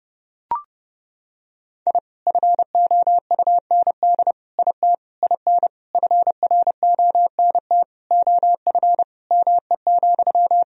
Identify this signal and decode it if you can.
Morse code, heard as a keyed tone, about 30 wpm, 700 Hz. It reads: I FOUND IT IN FRONT OF ME,
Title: Sentences from Top 1000 Words - Review Part 2 - 30wpm